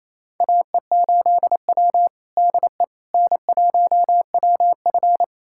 Morse code, heard as a keyed tone, about 28 wpm, 700 Hz. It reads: AE8W DE N1WF
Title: USA Callsign DE Callsign - 4 Characters - 28wpm